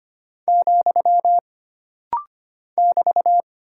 Morse code, heard as a keyed tone, about 25 wpm, 700 Hz. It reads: , <BT>
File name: Single Punctuation-Prosign - Rapid-Fire - 25wpm